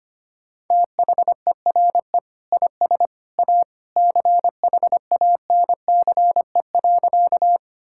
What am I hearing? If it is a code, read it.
Morse code, heard as a keyed tone, about 25 wpm, 700 Hz. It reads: THERE IS A CHANCE.
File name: practice / 25wpm / Sentences with 2-Word Phrases - 25wpm